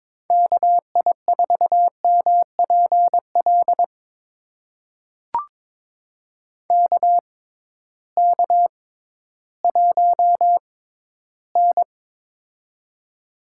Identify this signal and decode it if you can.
Morse code, heard as a keyed tone, about 22 wpm, spaced as for 13 wpm, 700 Hz. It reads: KI4MPL KK1N
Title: USA Call Signs - 6x Character Spacing - 22wpm